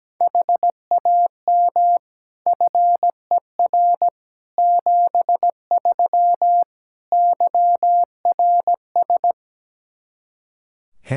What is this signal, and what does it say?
Morse code, heard as a keyed tone, about 17 wpm, 700 Hz. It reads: HAM FER 73 YRS
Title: Practice QSO - Exchange 3 - More Info - 17wpm